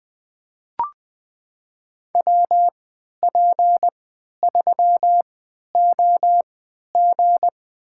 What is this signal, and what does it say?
Morse code, heard as a keyed tone, about 20 wpm, spaced as for 11 wpm, 700 Hz. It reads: WP3OG
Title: USA Call Signs - 5 Characters - 3x Character Spacing - 20wpm